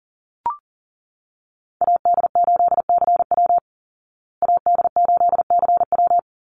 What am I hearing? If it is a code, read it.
Morse code, heard as a keyed tone, about 40 wpm, 700 Hz. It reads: AD8CW AD8CW
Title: Call Signs Repeated 3x - 40wpm